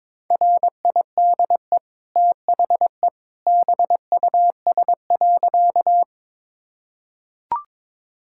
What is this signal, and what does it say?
Morse code, heard as a keyed tone, about 22 wpm, 690 Hz. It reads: RIDE THE BUS.
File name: practice / 22wpm / Easy Sentences with Hints - 22wpm